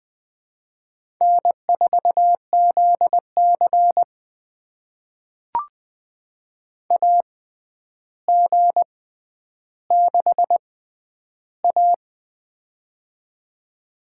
Morse code, heard as a keyed tone, about 20 wpm, spaced as for 10 wpm, 700 Hz. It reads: N4ZC AG6A
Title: USA Call Signs - 4 Characters - 6x Character Spacing - 20wpm